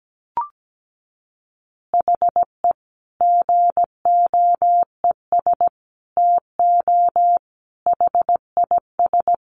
Morse code, heard as a keyed tone, about 17 wpm, 700 Hz. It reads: HE GOES TO HIS
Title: Easy Phrases 17wpm